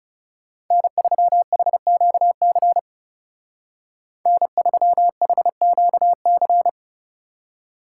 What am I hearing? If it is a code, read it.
Morse code, heard as a keyed tone, about 36 wpm, 700 Hz. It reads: N3HQC N3HQC
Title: Speed Racing - USA 1x3 Call Signs 30wpm